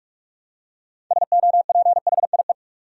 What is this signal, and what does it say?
Morse code, heard as a keyed tone, about 45 wpm, 700 Hz. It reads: SOPHIE
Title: USA Common Names - 45wpm